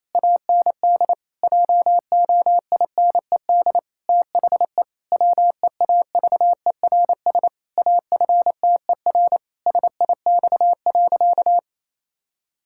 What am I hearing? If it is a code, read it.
Morse code, heard as a keyed tone, about 28 wpm, 700 Hz. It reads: AND JOINED THE WEAVERS AFTER SIX.
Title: Pangram Challenge v2 - 28wpm